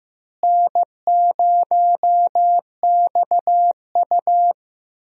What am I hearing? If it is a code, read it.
Morse code, heard as a keyed tone, about 15 wpm, 700 Hz. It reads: N0XU
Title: USA Call Signs - 4 Characters - Repeated 2x - 15wpm